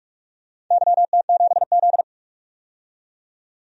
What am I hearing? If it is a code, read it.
Morse code, heard as a keyed tone, about 45 wpm, 700 Hz. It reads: YT7Z
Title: DXCC-100 Call Signs - 4 Characters - 45wpm